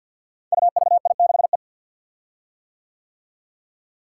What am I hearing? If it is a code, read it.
Morse code, heard as a keyed tone, about 52 wpm, 700 Hz. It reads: UVI6E
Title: Code Groups - 5 Letter:Numbers - 50wpm